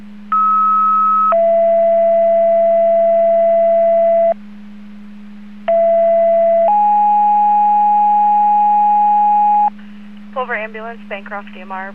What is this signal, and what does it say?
Two-tone page, heard as a one-second tone, then a three-second tone, about 1290 then 670 Hz, about 670 then 830 Hz, with speech. It is heard from 0.3 to 9.7 s.